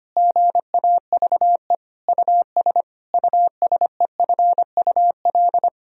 Morse code, heard as a keyed tone, about 25 wpm, 700 Hz. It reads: GAVE US USEFUL
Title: Sentences from Top 1000 Words - Review Part 4 - 25wpm